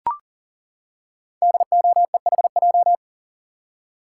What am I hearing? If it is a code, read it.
Morse code, heard as a keyed tone, about 39 wpm, 690 Hz. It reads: DOEHJ
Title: Code Groups - 5 Letters - 40wpm